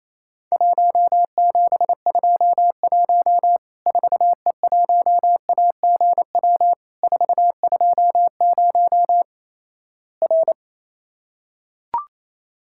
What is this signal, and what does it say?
Morse code, heard as a keyed tone, about 28 wpm, 700 Hz. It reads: 1721 4E1AGW 420 R
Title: Worked All Europe DX Contest - QTC Exchange - 28wpm